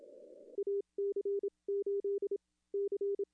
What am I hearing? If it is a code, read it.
Morse code, heard as a keyed tone, about 27 wpm, spaced as for 20 wpm, 410 Hz. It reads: AC8C
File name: contests / arrl-160-k5tr-op-w5tn -0300